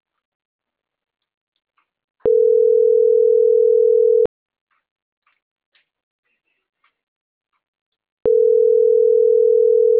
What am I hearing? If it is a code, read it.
Ringback tone (North America), 440 + 480 Hz.